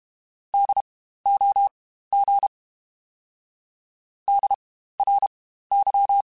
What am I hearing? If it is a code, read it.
Morse code, heard as a keyed tone, about 32 wpm, spaced as for 15 wpm, 800 Hz. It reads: DOG DRY
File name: Spaced Alphas / Spaced Alphas mp3s / Ds sp @30